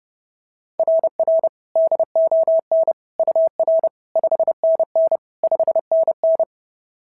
Morse code, heard as a keyed tone, about 32 wpm, 690 Hz. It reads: RR DON UR 5NN 5NN
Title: SOTA QSO Practice 30wpm